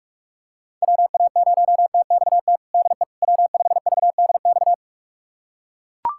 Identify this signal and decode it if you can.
Morse code, heard as a keyed tone, about 41 wpm, 700 Hz. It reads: WA0TXT DE W5VDX
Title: USA Callsign DE Callsign 45wpm